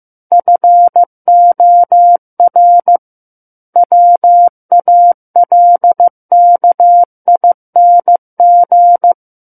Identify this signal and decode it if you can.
Morse code, heard as a keyed tone, about 15 wpm, 700 Hz. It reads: FOR WALKING
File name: Chapter 6 - 15 WPM